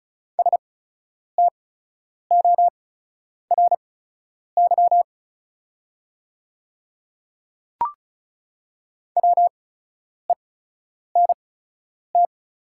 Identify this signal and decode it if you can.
Morse code, heard as a keyed tone, about 36 wpm, spaced as for 10 wpm, 700 Hz. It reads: STORY WENT
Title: 8x Character Spacing - Top 300 Words - 35wpm